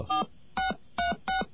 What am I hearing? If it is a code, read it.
DTMF, heard as telephone touch tones, tones of about 120 ms. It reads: 4636
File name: mp3files / off the hook